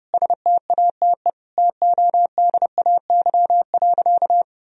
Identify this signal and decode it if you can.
Morse code, heard as a keyed tone, about 30 wpm, 700 Hz. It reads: STATE TODAY.
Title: Sentences from Top 1000 Words - Review Part 2 - 30wpm